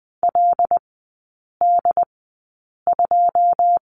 Morse code, heard as a keyed tone, about 20 wpm, spaced as for 8 wpm, 700 Hz. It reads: LD2
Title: 107 Lesson 18e - Letter m - Three Characters 20wpm